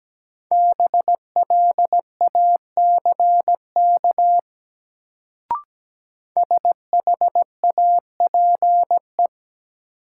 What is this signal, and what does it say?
Morse code, heard as a keyed tone, about 17 wpm, 700 Hz. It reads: BLACK SHAPE
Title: Sets of 5 Words Encore - New Word Drill - 17wpm